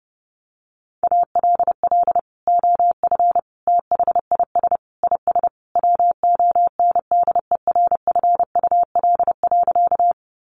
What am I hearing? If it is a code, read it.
Morse code, heard as a keyed tone, about 30 wpm, 700 Hz. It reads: ALL OF THIS IS WONDERFUL.
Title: Sentences from Top 1100 Words 30wpm